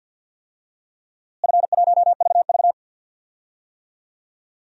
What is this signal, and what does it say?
Morse code, heard as a keyed tone, about 46 wpm, 700 Hz. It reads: F1VV
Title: DXCC-100 Call Signs - 4 Characters - Repeated 3x - 50wpm